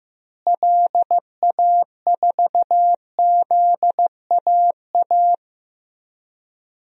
Morse code, heard as a keyed tone, about 15 wpm, 700 Hz. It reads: LA4ZAA
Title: DXCC-100 Call Signs - 6 Characters - Repeated 2x - 15wpm